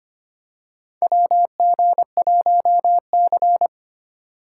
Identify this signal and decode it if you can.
Morse code, heard as a keyed tone, about 25 wpm, 700 Hz. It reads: WG1C
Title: USA Call Signs - 4 Characters - 4x Character Spacing - 25wpm